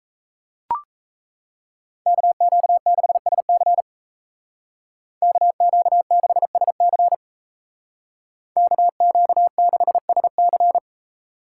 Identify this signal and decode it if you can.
Morse code, heard as a keyed tone, about 37 wpm, 700 Hz. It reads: KQ6SC KQ6SC KQ6SC